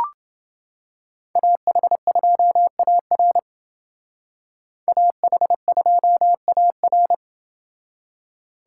Morse code, heard as a keyed tone, about 28 wpm, 700 Hz. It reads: AH2AR AH2AR